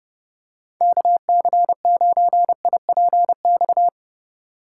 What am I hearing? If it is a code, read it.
Morse code, heard as a keyed tone, about 30 wpm, 700 Hz. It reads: KC9IPX
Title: USA Call Signs - 6 Characters - 6x Character Spacing - 30wpm